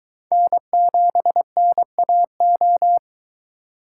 Morse code, heard as a keyed tone, about 23 wpm, 700 Hz. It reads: N7NAO